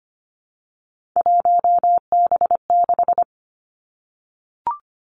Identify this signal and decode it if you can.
Morse code, heard as a keyed tone, about 25 wpm, 700 Hz. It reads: 1B6
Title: Three Challenging Letter-Number Combinations 25wpm